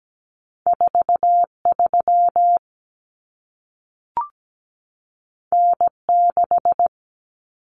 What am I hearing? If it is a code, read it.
Morse code, heard as a keyed tone, about 17 wpm, 700 Hz. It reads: 43 N6
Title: Common Cut Numbers - 2-Digit 17wpm